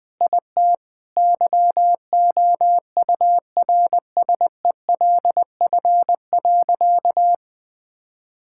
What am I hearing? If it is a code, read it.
Morse code, heard as a keyed tone, about 20 wpm, 700 Hz. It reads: IT YOURSELF.
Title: Sentences with 7-Word Phrases - 20wpm